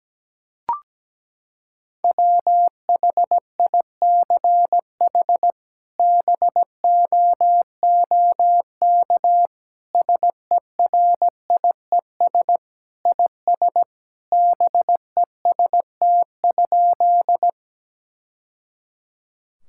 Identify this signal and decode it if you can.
Morse code, heard as a keyed tone, about 17 wpm, 700 Hz. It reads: WHICH BOOK SERIES IS BEST?
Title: Sentences from Top 1000 Words - Review Part 5 - 17wpm